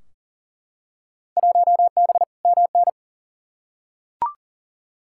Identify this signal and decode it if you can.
Morse code, heard as a keyed tone, about 40 wpm, 700 Hz. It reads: 1B MN
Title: Field Day Exchanges Encore - 40wpm